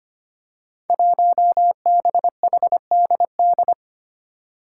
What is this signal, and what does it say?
Morse code, heard as a keyed tone, about 25 wpm, 700 Hz. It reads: 1BHDD
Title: Five Challenging Letter-Number Combinations 25wpm